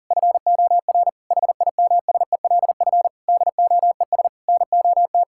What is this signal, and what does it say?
Morse code, heard as a keyed tone, about 40 wpm, 700 Hz. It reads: FOR HIMSELF DOES NOT
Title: Sentences from Top 1600 Words 40wpm